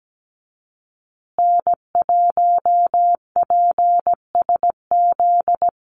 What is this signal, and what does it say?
Morse code, heard as a keyed tone, about 17 wpm, 700 Hz. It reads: N1PSZ
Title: USA Call Signs Repeated 3x 17wpm